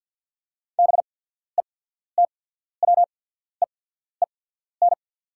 Morse code, heard as a keyed tone, about 48 wpm, spaced as for 14 wpm, 690 Hz. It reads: BETWEEN
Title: 8x Character Spacing - Top 500 Words - 50wpm